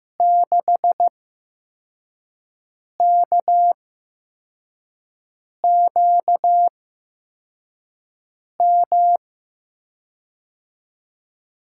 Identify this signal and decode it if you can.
Morse code, heard as a keyed tone, about 15 wpm, spaced as for 4 wpm, 700 Hz. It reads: 6KQM
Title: USA Call Signs - 8x Character Spacing - 15wpm